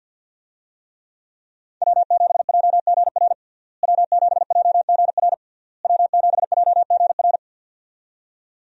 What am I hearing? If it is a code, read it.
Morse code, heard as a keyed tone, about 48 wpm, 700 Hz. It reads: W7JGR W7JGR W7JGR
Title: USA Call Signs - 5 Characters - Repeated 3x - 50wpm